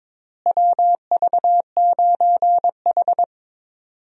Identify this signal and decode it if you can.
Morse code, heard as a keyed tone, about 21 wpm, 700 Hz. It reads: WV9H
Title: USA Call Signs - 4 Characters - Repeated 3x - 22wpm